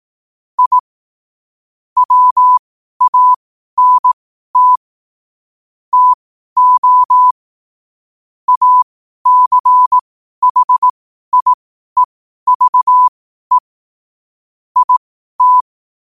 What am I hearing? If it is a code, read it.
Morse code, heard as a keyed tone, about 18 wpm, spaced as for 12 wpm, 1000 Hz. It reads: I WANT TO ACHIEVE IT
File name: qotdfiles / 12wpm